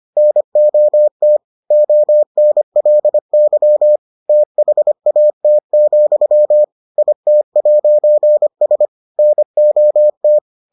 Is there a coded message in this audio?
Yes: Morse code from 0.2 to 10.4 s, heard as a keyed tone, about 25 wpm, 600 Hz. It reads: NOT ONLY THAT, IT'S NOT